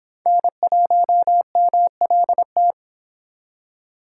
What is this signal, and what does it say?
Morse code, heard as a keyed tone, about 24 wpm, 700 Hz. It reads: N1MLT